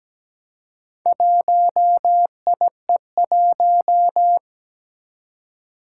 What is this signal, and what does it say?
Morse code, heard as a keyed tone, about 17 wpm, 700 Hz. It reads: 1IE1
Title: Four Challenging Letter-Number Combinations 17wpm